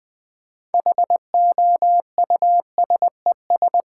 Morse code, heard as a keyed tone, about 20 wpm, 700 Hz. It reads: HOUSES
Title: Sentences with 5-Word Phrases - 20wpm